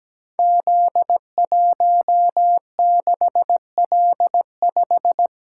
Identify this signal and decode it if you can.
Morse code, heard as a keyed tone, about 17 wpm, 700 Hz. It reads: Z16L5
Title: Code Groups - 5 Letter:Numbers - 17wpm